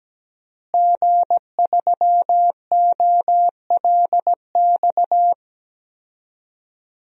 Morse code, heard as a keyed tone, about 17 wpm, 700 Hz. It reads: G3OLX
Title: DXCC-100 Call Signs - 5 Characters - Repeated 3x - 17wpm